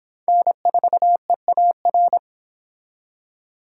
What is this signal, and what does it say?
Morse code, heard as a keyed tone, about 26 wpm, 700 Hz. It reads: N4EAR